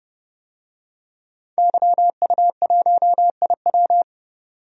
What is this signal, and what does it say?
Morse code, heard as a keyed tone, about 30 wpm, 700 Hz. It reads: YU1IW
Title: DXCC-100 Call Signs - 5 Characters - 30wpm